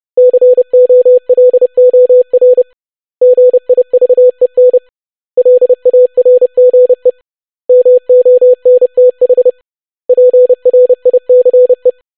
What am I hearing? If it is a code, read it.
Morse code, heard as a keyed tone, about 30 wpm, 500 Hz. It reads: COLOR GIVEN LARGE MONTH PRICE
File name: basic500-25wpm